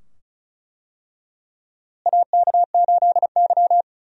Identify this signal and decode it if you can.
Morse code, heard as a keyed tone, about 35 wpm, 700 Hz. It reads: AK8Y
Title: USA Call Signs - 4 Characters - Repeated 2x - 35wpm